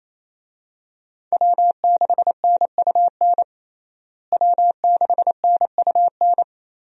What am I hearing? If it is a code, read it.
Morse code, heard as a keyed tone, about 28 wpm, 700 Hz. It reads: W6NUN W6NUN
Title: USA Call Signs - 5 Characters - Repeated 2x - 28wpm